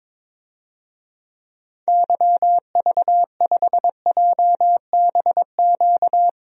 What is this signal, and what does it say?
Morse code, heard as a keyed tone, about 22 wpm, 700 Hz. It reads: YV5JBQ
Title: DXCC-100 Call Signs - 6 Characters - Repeated 2x - 22wpm